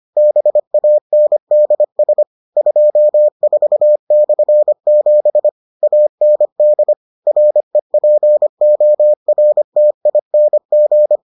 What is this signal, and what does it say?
Morse code, heard as a keyed tone, about 25 wpm, 600 Hz. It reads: BANDS 24/7 AND REPORTING